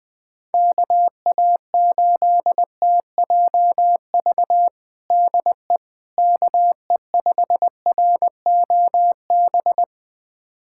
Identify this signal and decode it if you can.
Morse code, heard as a keyed tone, about 20 wpm, 700 Hz. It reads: KA8TJV DE KE5ROB
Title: USA Callsign DE Callsign 20wpm